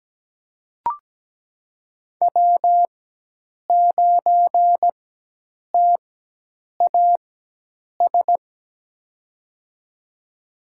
Morse code, heard as a keyed tone, about 17 wpm, spaced as for 8 wpm, 700 Hz. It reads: W9TAS
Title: USA Call Signs - 4x Character Spacing - 17wpm